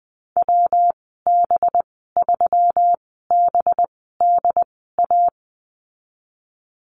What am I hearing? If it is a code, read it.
Morse code, heard as a keyed tone, about 20 wpm, spaced as for 15 wpm, 700 Hz. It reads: WB3BDA
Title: 200 Lesson 32g - Letter k - Call Signs 20wpm